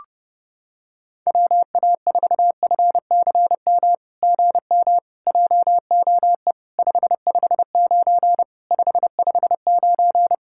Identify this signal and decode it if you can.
Morse code, heard as a keyed tone, about 30 wpm, 700 Hz. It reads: WA4FCM GM JOE 559 559
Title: SOTA QSO Practice 30wpm